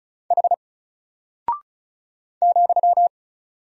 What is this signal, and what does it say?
Morse code, heard as a keyed tone, about 35 wpm, 700 Hz. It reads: H ,
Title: Single Character (Letter:Number:Punctuation:Prosign) Alt 35wpm